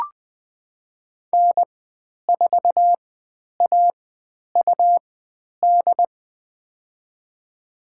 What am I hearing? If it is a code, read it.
Morse code, heard as a keyed tone, about 20 wpm, spaced as for 10 wpm, 700 Hz. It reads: N4AUD
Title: 109 Lesson 18g - Letter m - Call Signs 20wpm